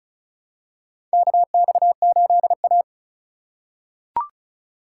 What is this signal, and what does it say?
Morse code, heard as a keyed tone, about 36 wpm, 700 Hz. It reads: KX8A
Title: USA Call Signs - 4 Characters - 2x Character Spacing - 35wpm